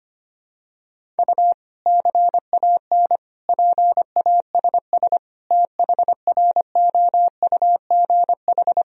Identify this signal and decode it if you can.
Morse code, heard as a keyed tone, about 25 wpm, 700 Hz. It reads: U CAN PASS THROUGH
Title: Sentences from Top 400 Words and Abbreviations 25wpm